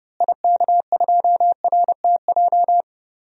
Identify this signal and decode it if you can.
Morse code, heard as a keyed tone, about 30 wpm, 700 Hz. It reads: IK2RTJ
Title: DXCC-100 Call Signs - 6 Characters - Repeated 2x - 30wpm